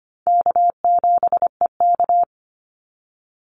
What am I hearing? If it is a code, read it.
Morse code, heard as a keyed tone, about 25 wpm, 700 Hz. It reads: K7EK
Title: Easy Call Signs - 25wpm